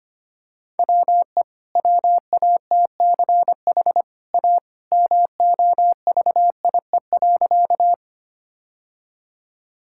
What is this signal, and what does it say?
Morse code, heard as a keyed tone, about 25 wpm, 690 Hz. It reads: WE WATCH A MOVIE.
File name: Easy Sentences with Hints - 25wpm